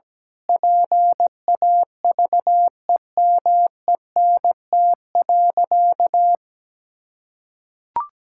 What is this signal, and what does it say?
Morse code, heard as a keyed tone, about 17 wpm, 700 Hz. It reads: PAVEMENT.
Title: Pangram Challenge v2 - 17wpm